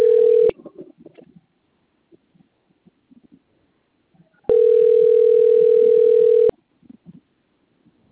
Ringback tone (North America), 440 + 480 Hz.